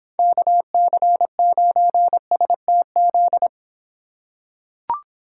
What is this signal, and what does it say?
Morse code, heard as a keyed tone, about 25 wpm, 700 Hz. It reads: KC9STZ